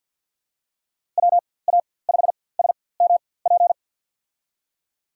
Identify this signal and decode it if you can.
Morse code, heard as a keyed tone, about 51 wpm, spaced as for 23 wpm, 700 Hz. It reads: WA5SMP
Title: USA Call Signs - 6 Characters - 4x Character Spacing - 50wpm